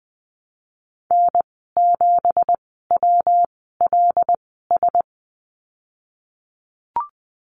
Morse code, heard as a keyed tone, about 20 wpm, spaced as for 15 wpm, 700 Hz. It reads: N7WLS
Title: 179 Lesson 29g - Slash - Call Signs 20wpm